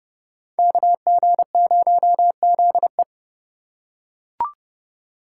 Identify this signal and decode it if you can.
Morse code, heard as a keyed tone, about 29 wpm, 700 Hz. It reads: KG0ZE